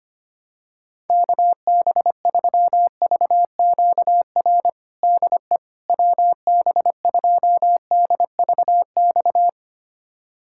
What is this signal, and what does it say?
Morse code, heard as a keyed tone, about 25 wpm, 700 Hz. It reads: KB3VQR DE WB2DVX
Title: USA Callsign DE Callsign - 6 Characters - 25wpm